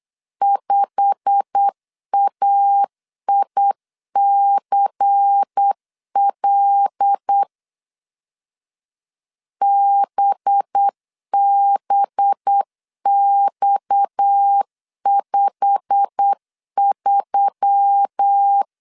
Morse code, heard as a keyed tone, about 8 wpm, 800 Hz. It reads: 5AICL BBX53
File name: LEZIONI / 10 SIGNS / 10D COMA 8wpm 3749